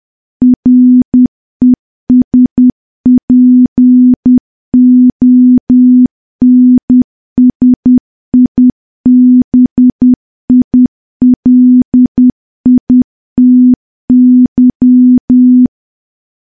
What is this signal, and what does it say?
Morse code, heard as a keyed tone, about 10 wpm, 250 Hz. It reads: RESPONSIBILITY
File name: morse / morse-training-10WPM